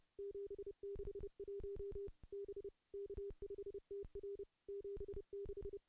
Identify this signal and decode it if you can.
Morse code, heard as a keyed tone, about 30 wpm, 400 Hz. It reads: 761 B K5TR 76
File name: contests / CW-SS-k5tr- -0600